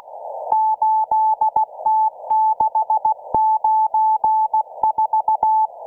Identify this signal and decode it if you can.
Morse code, heard as a keyed tone, about 16 wpm, 870 Hz. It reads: 8T694